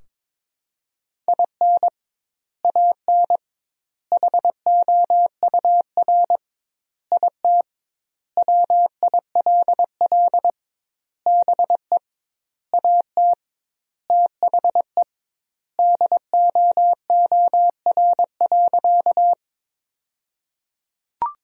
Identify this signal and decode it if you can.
Morse code, heard as a keyed tone, about 22 wpm, 700 Hz. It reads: IN AN HOUR IT WILL BE AT THE DOOR.
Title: 2x Word-Spacing - Sentences from Top 400 Words - 22wpm